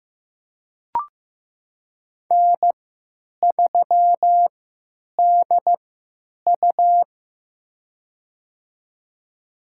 Morse code, heard as a keyed tone, about 15 wpm, spaced as for 9 wpm, 700 Hz. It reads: N3DU
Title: USA Call Signs - 4 Characters - 3x Character Spacing - 15wpm